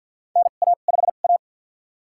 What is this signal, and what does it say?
Morse code, heard as a keyed tone, about 53 wpm, spaced as for 37 wpm, 700 Hz. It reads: NA5A
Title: USA Call Signs - 4 Characters - 2x Character Spacing - 50wpm